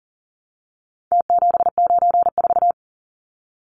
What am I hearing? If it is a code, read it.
Morse code, heard as a keyed tone, about 40 wpm, 700 Hz. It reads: T794
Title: Numbers - Contest Style - 4-Digit 40wpm